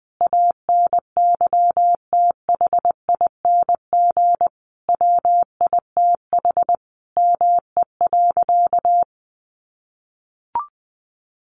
Morse code, heard as a keyed tone, about 20 wpm, 700 Hz. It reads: ANYTHING WITH ME.
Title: Sentences from Top 1000 Words 20wpm